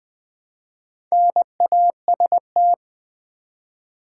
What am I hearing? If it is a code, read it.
Morse code, heard as a keyed tone, about 20 wpm, 700 Hz. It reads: NAST
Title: Build A Word - 1 Letter At A Time - 5-Letter Words - 20wpm